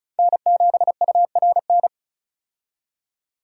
Morse code, heard as a keyed tone, about 36 wpm, 700 Hz. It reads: N7URN
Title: USA Call Signs - 5 Characters - 35wpm